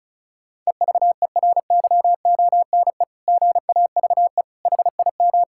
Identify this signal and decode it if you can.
Morse code, heard as a keyed tone, about 34 wpm, 700 Hz. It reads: EVERYONE GAVE HIM
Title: Sentences from Top 1000 Words - Review Part 6 - 35wpm